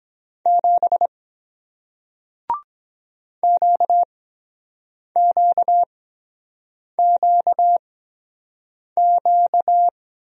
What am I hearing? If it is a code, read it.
Morse code, heard as a keyed tone, about 23 wpm, spaced as for 7 wpm, 700 Hz. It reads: 7 QQQQ